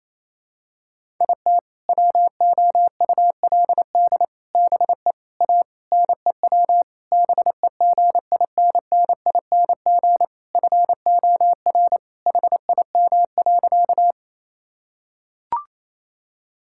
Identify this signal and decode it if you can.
Morse code, heard as a keyed tone, about 28 wpm, 700 Hz. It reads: IT WOULD BE A NEW BEGINNING FOR HIM.
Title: Sentences from Top 1000 Words - Review Part 5 - 28wpm